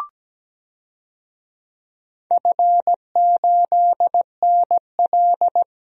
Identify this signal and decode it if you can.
Morse code, heard as a keyed tone, about 17 wpm, 700 Hz. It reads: F8NL
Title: DXCC-100 Call Signs - 4 Characters - 17wpm